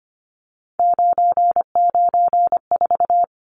Morse code, heard as a keyed tone, about 25 wpm, 700 Hz. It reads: 994
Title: Numbers - 3-Digit 25wpm